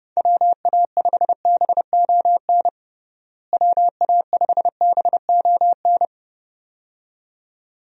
Morse code, heard as a keyed tone, about 30 wpm, 700 Hz. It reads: WA5BON WA5BON
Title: USA Call Signs - 6 Characters - Repeated 2x - 30wpm